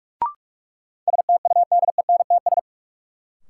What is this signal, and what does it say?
Morse code, heard as a keyed tone, about 46 wpm, 700 Hz. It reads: STUDENTS
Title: Sets of 5 Words Encore - New Word Drill - 45wpm